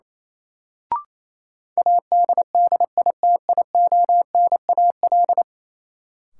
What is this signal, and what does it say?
Morse code, heard as a keyed tone, about 28 wpm, 700 Hz. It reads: ADDITIONAL
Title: Sentences from Top 1000 Words - Review Part 4 - New Word Drill - 28wpm